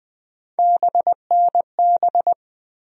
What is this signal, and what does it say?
Morse code, heard as a keyed tone, about 20 wpm, 700 Hz. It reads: BNB
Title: USA Call Sign Postfixes - 3 Letters 20wpm